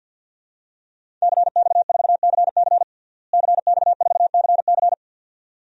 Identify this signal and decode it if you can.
Morse code, heard as a keyed tone, about 50 wpm, 700 Hz. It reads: CX4CC CX4CC
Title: DXCC-100 Call Signs - 5 Characters - Repeated 2x - 50wpm